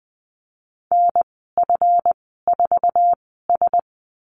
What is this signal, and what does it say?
Morse code, heard as a keyed tone, about 20 wpm, spaced as for 14 wpm, 700 Hz. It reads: NF4S
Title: 140 Lesson 23g - Letter f - Call Signs 20wpm